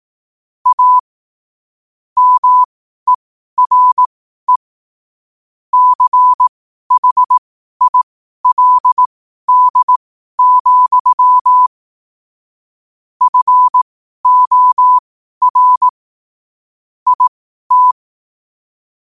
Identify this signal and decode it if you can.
Morse code, heard as a keyed tone, about 18 wpm, spaced as for 12 wpm, 1000 Hz. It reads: A MERE CHILD, FOR IT
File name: qotdfiles / 12wpm